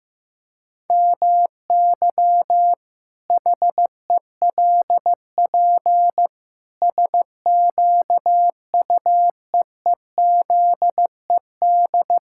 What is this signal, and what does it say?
Morse code, heard as a keyed tone, about 16 wpm, 700 Hz. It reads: MY HELP SQUEEZED
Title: Pangram Challenge v2 - 15wpm